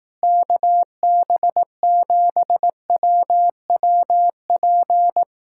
Morse code, heard as a keyed tone, about 18 wpm, 700 Hz. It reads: KB7WWP
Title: Speed-Racing Common QSO Phrases Encore - 15wpm